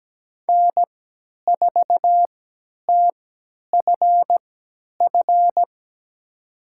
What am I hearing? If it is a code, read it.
Morse code, heard as a keyed tone, about 17 wpm, spaced as for 10 wpm, 700 Hz. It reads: N4TFF
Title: USA Call Signs - 3x Character Spacing - 17wpm